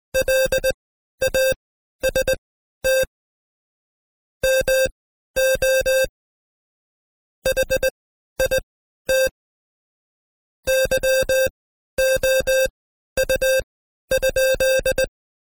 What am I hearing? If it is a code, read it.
Morse code, heard as a keyed tone, about 19 wpm, spaced as for 12 wpm, 530 Hz. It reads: LAST MO HIT YOU?
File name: Code Quick / AudioDownld2 / Seg132-Stepper5 11WPM